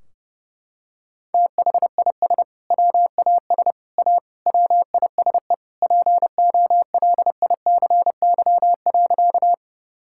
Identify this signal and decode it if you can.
Morse code, heard as a keyed tone, about 28 wpm, 700 Hz. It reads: THIS WAS A WISE POLICY.
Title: Sentences from Top 1600 Words 30wpm